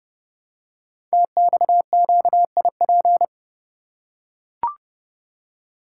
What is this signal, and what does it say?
Morse code, heard as a keyed tone, about 29 wpm, 700 Hz. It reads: TXQIP
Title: Code Groups - 5 Letters - 30wpm